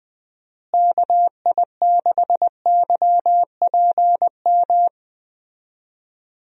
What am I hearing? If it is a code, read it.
Morse code, heard as a keyed tone, about 20 wpm, 700 Hz. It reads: KI6YPM
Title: USA Call Signs - 6 Characters - Repeated 3x - 20wpm